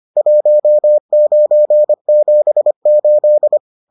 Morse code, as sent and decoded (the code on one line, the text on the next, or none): .---- ----. --... ---..
1978